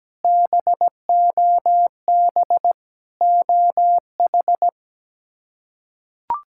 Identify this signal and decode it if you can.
Morse code, heard as a keyed tone, about 17 wpm, 700 Hz. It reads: BOB OH
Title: K1USN SST QSO Practice - 17wpm